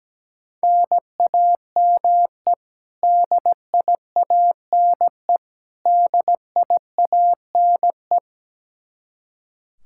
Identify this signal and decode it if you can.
Morse code, heard as a keyed tone, about 17 wpm, 700 Hz. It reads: NAME DIANE DIANE
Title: Practice QSO - Exchange 1 - Beginning - 17wpm